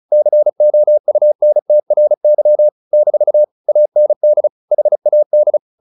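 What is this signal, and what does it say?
Morse code, heard as a keyed tone, about 34 wpm, 600 Hz. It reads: COUNTRY - AND HAD
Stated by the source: Morse practice at 35 wpm from CW med Gnister, sending Ken Follett's novel Never in English.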